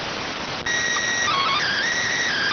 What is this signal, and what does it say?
SSTV (slow-scan television), PD 290.